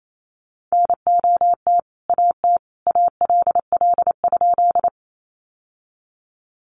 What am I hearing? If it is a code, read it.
Morse code, heard as a keyed tone, about 28 wpm, 700 Hz. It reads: NOT AT ALL?
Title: Sentences from Top 600 Words 28wpm